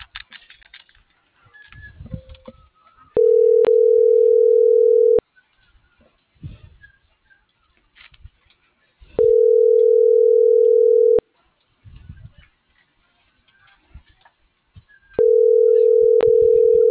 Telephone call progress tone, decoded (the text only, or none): ringback tone